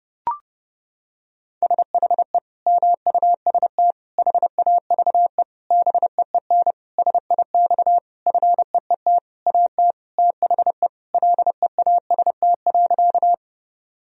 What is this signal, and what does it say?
Morse code, heard as a keyed tone, about 30 wpm, 700 Hz. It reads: SHE MUST HAVE BEEN SIX FEET AT THE LEAST.
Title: Sentences from Top 1000 Words - Review Part 2 - 30wpm